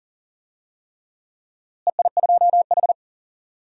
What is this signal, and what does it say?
Morse code, heard as a keyed tone, about 40 wpm, 700 Hz. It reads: EI2H
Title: DXCC-100 Call Signs - 4 Characters - Repeated 3x - 40wpm